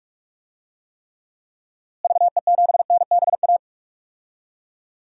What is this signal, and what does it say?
Morse code, heard as a keyed tone, about 46 wpm, 700 Hz. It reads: VE7NBA
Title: DXCC-100 Call Signs - 6 Characters - Repeated 2x - 45wpm